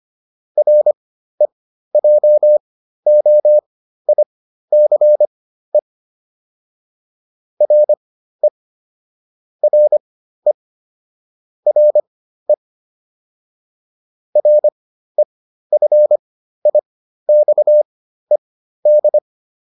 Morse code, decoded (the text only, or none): REJOICE RE RE RE REFIXED